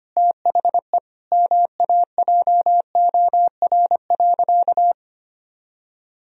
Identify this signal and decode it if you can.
Morse code, heard as a keyed tone, about 25 wpm, 700 Hz. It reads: THE MAJOR.
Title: Sentences from Top 1000 Words - Review Part 4 - 25wpm